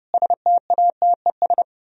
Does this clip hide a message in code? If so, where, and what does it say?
Yes: Morse code from 0.1 to 1.6 s, heard as a keyed tone, about 40 wpm, 700 Hz. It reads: STATES